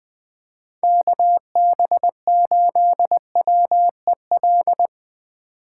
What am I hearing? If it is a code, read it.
Morse code, heard as a keyed tone, about 20 wpm, 700 Hz. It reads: KB8WEL
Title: USA Call Signs - 6 Characters - 4x Character Spacing - 20wpm